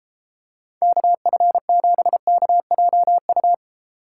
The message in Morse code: KF7KJU

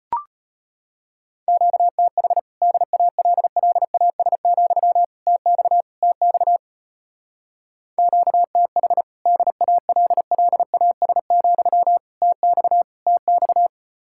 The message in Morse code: QTH DALLAS, TX TX QTH DALLAS, TX TX

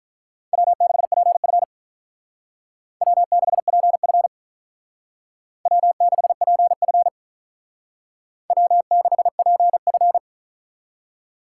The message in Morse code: W6PF W6PF W6PF W6PF

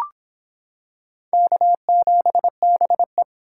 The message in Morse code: K7BE

